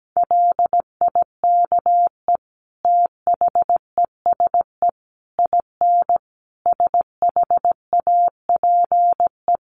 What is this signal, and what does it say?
Morse code, heard as a keyed tone, about 17 wpm, 700 Hz. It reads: LIKE THESE IN SHAPE
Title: Sentences from Top 500 Words 17wpm